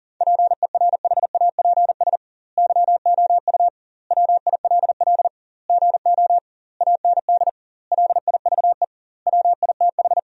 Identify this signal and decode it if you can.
Morse code, heard as a keyed tone, about 40 wpm, 700 Hz. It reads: PERHAPS YOU WILL GO AND LIVE WITH
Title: 2x Word-Spacing - Sentences from Top 500 Words - 40wpm